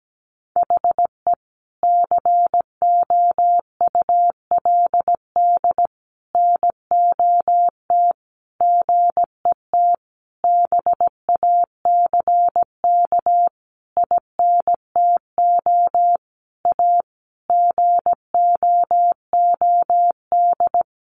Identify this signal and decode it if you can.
Morse code, heard as a keyed tone, about 17 wpm, 700 Hz. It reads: HE COULD NOT GET BACK INTO A GOOD
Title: Sentences from Top 1100 Words 17wpm